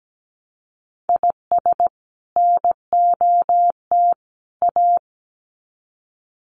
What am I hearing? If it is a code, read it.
Morse code, heard as a keyed tone, about 17 wpm, 700 Hz. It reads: IS NOT A
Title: Sets of 3 Words 17wpm